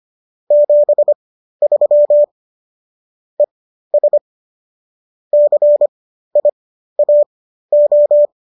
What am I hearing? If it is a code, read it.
Morse code, heard as a keyed tone, about 25 wpm, spaced as for 13 wpm, 590 Hz. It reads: 73 ES CIAO